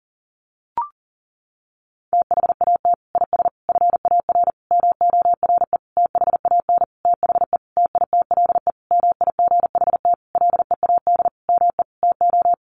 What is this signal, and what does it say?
Morse code, heard as a keyed tone, about 40 wpm, 700 Hz. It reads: THAT IS FAR MORE THAN THE TITLE MIGHT LEAD ME TO
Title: Sentences from Top 1200 Words 40wpm